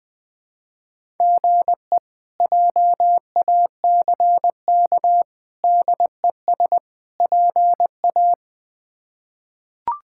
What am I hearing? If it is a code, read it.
Morse code, heard as a keyed tone, about 20 wpm, 700 Hz. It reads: GE JACK DES PA
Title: K1USN SST QSO Practice - 20wpm